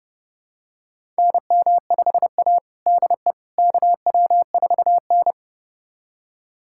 Morse code, heard as a keyed tone, about 30 wpm, 700 Hz. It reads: NM5A DE KW4N
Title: USA Callsign DE Callsign - 4 Characters - 30wpm